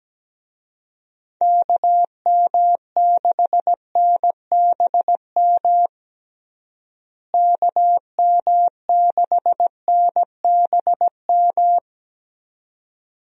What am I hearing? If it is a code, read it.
Morse code, heard as a keyed tone, about 17 wpm, 700 Hz. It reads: KM6NBM KM6NBM
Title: USA Call Signs - 6 Characters - Repeated 2x - 17wpm